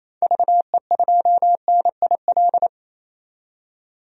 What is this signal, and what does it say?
Morse code, heard as a keyed tone, about 28 wpm, 700 Hz. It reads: VE2NIL
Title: DXCC-100 Call Signs - 6 Characters - 28wpm